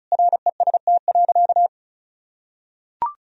Morse code, heard as a keyed tone, about 36 wpm, 700 Hz. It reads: REST.
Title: Sentences from Top 1000 Words - Review Part 4 - 35wpm